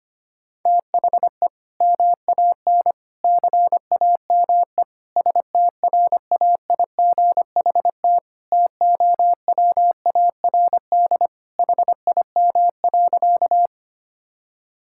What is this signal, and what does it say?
Morse code, heard as a keyed tone, about 25 wpm, 700 Hz. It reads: THE MAN CAME STRAIGHT TOWARD HIM.
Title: Sentences from Top 1000 Words - Review Part 6 - 25wpm